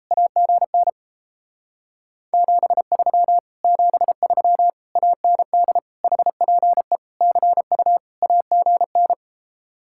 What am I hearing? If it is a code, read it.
Morse code, heard as a keyed tone, about 37 wpm, 700 Hz. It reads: AGN 73 73 AND HPE CU AGN